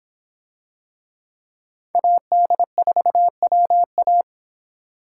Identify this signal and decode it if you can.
Morse code, heard as a keyed tone, about 24 wpm, 700 Hz. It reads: AD4WA